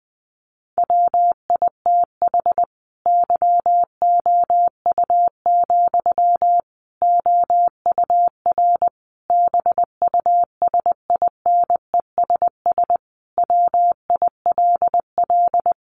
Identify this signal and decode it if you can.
Morse code, heard as a keyed tone, about 20 wpm, 700 Hz. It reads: WITH YOU, OUR BUSINESS WILL
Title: Sentences from Top 1400 Words 20wpm